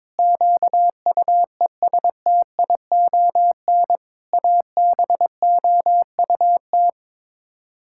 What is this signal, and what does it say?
Morse code, heard as a keyed tone, about 21 wpm, 700 Hz. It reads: QUESTION ABOUT
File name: practice / 22wpm / Sets of 6 Words - 22wpm